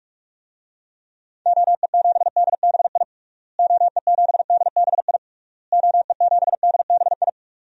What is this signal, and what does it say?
Morse code, heard as a keyed tone, about 41 wpm, 700 Hz. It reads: OE7DBI OE7DBI OE7DBI
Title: DXCC-100 Call Signs - 6 Characters - Repeated 3x - 45wpm